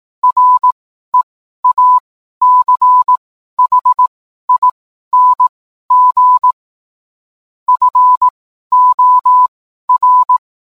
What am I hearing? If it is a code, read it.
Morse code, heard as a keyed tone, about 18 wpm, spaced as for 12 wpm, 1000 Hz. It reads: REACHING FOR